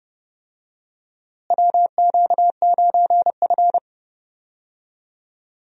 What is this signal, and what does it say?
Morse code, heard as a keyed tone, about 30 wpm, 700 Hz. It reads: WQ9F
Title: USA Call Signs - 4 Characters - Repeated 2x - 30wpm